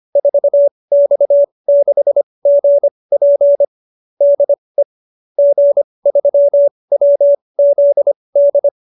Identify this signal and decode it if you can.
Morse code, heard as a keyed tone, about 25 wpm, spaced as for 20 wpm, 570 Hz. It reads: 4X6GP DE G3WZD
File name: QSO310 20